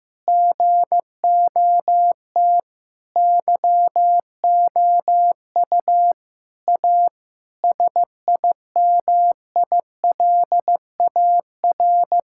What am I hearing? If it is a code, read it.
Morse code, heard as a keyed tone, about 15 wpm, 700 Hz. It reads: GOT YOU A SIMILAR
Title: Sentences from Top 1000 Words - Review Part 5 - 15wpm